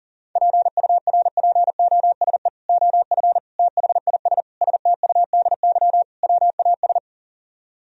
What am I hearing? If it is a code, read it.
Morse code, heard as a keyed tone, about 51 wpm, 700 Hz. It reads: PURPOSE OF THIS STUDY WAS